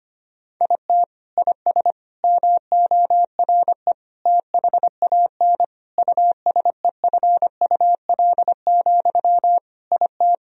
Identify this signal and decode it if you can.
Morse code, heard as a keyed tone, about 25 wpm, 700 Hz. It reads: IT IS MORE THAN USEFUL, IT
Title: Sentences from Top 1000 Words - Review Part 4 - 25wpm